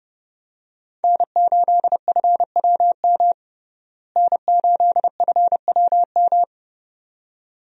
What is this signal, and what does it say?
Morse code, heard as a keyed tone, about 32 wpm, 700 Hz. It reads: N8FWM N8FWM
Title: USA Call Signs - 5 Characters - Repeated 2x - 30wpm